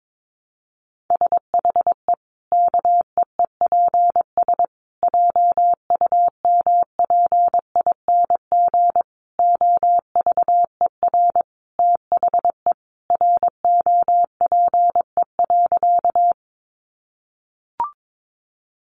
Morse code, heard as a keyed tone, about 22 wpm, 700 Hz. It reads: SHE KEEPS JUMPING OVER THE ROPE.
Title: Easy Phrases 22wpm